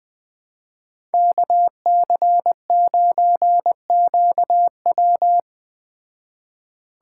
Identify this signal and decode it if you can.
Morse code, heard as a keyed tone, about 20 wpm, 700 Hz. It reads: KC9QW